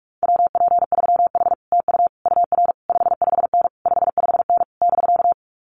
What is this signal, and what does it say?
Morse code, heard as a keyed tone, about 46 wpm, 700 Hz. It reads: WP3H TU UR 55N 55N <BK>
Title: Common QSO Phrases Encore 45wpm